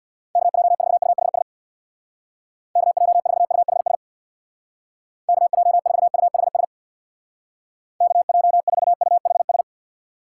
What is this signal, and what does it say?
Morse code, heard as a keyed tone, about 61 wpm, 700 Hz. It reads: KJ4UHS KJ4UHS KJ4UHS KJ4UHS
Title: Speed Racing - USA 2x3 Call Signs 50wpm